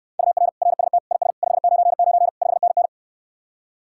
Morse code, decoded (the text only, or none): UR RST IS 599 5NN